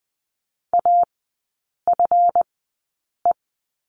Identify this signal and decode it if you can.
Morse code, heard as a keyed tone, about 20 wpm, spaced as for 8 wpm, 700 Hz. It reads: AFE
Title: 138 Lesson 23e - Letter f - Three Characters 20wpm